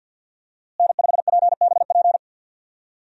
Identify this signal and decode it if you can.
Morse code, heard as a keyed tone, about 53 wpm, 700 Hz. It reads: N5PBP
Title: USA Call Signs - 5 Characters - Repeated 3x - 50wpm